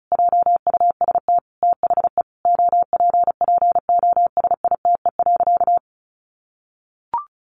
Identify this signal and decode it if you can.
Morse code, heard as a keyed tone, about 34 wpm, 700 Hz. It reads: JUST THE OPPOSITE.
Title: Sentences from Top 1300 Words 35wpm